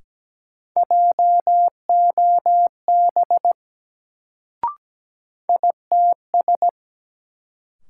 Morse code, heard as a keyed tone, about 17 wpm, 700 Hz. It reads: JOB ITS
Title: Sentences from Top 600 Words - New Word Drill - 17wpm